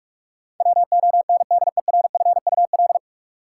Morse code, heard as a keyed tone, about 46 wpm, 700 Hz. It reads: WONDERFUL
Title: Sentences from Top 1100 Words - New Word Drill - 45wpm